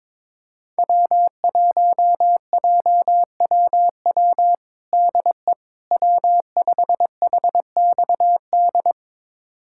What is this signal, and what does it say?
Morse code, heard as a keyed tone, about 22 wpm, 700 Hz. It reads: W1JWW DE W5HXD
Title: USA Callsign DE Callsign 22wpm